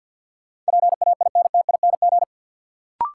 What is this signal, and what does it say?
Morse code, heard as a keyed tone, about 53 wpm, 700 Hz. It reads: PAINTING
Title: Sentences from Top 1200 Words - New Word Drill Alt - 50wpm